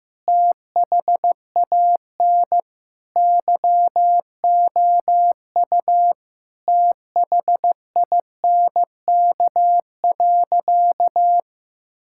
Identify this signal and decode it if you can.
Morse code, heard as a keyed tone, about 15 wpm, 700 Hz. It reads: THAN YOU THINK.
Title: Sentences from Top 1000 Words - Review Part 6 - 15wpm